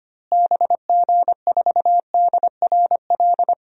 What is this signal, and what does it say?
Morse code, heard as a keyed tone, about 25 wpm, 700 Hz. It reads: BG4DRL
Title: DXCC-100 Call Signs - 6 Characters - Repeated 3x - 25wpm